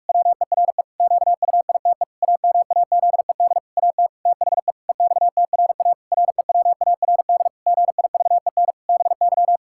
Morse code, heard as a keyed tone, about 46 wpm, 700 Hz. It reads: WERE QUITE AMAZED AT THE EXTRA REWARD GIVEN BY
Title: Pangram Challenge v2 - 45wpm